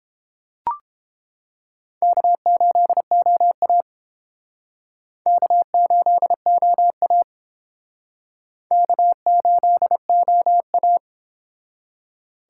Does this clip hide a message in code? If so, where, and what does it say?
Yes: Morse code from 2.0 to 11.0 s, heard as a keyed tone, about 29 wpm, 700 Hz. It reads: K8OA K8OA K8OA